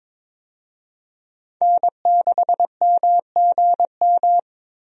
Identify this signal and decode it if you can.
Morse code, heard as a keyed tone, about 22 wpm, 700 Hz. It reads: N6MGM
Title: USA Call Signs - 5 Characters - Repeated 3x - 22wpm